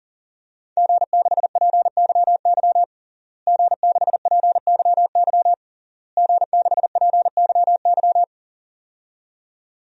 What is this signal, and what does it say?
Morse code, heard as a keyed tone, about 40 wpm, 700 Hz. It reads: G6PYY G6PYY G6PYY
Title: DXCC-100 Call Signs - 5 Characters - Repeated 3x - 40wpm